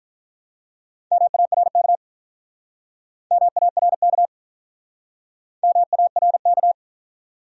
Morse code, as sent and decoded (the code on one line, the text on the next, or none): -- .- .-. -.- / -- .- .-. -.- / -- .- .-. -.-
MARK MARK MARK